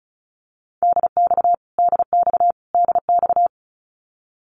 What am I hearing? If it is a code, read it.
Morse code, heard as a keyed tone, about 34 wpm, 700 Hz. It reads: DX DX DX
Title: Abbreviations Repeated 3x - 35wpm